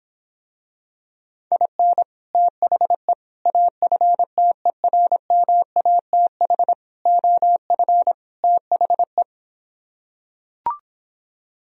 Morse code, heard as a keyed tone, about 26 wpm, 700 Hz. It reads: IN THE AFTERMATH OF THE